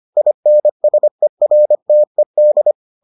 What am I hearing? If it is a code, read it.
Morse code, heard as a keyed tone, about 25 wpm, 600 Hz. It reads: INSERTED